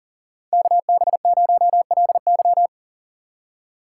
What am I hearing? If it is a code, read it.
Morse code, heard as a keyed tone, about 49 wpm, 700 Hz. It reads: KB0LY
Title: Speed Racing - USA 2x2 Call Signs 40wpm